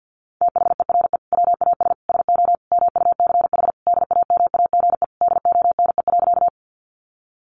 Morse code, heard as a keyed tone, about 50 wpm, 700 Hz. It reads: THERE WAS SO MUCH DAMAGE DONE.
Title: Sentences from Top 1100 Words 50wpm